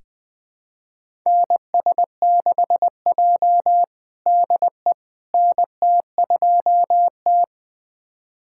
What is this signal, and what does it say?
Morse code, heard as a keyed tone, about 20 wpm, 700 Hz. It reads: NS6J DE NT2T
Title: USA Callsign DE Callsign - 4 Characters - 20wpm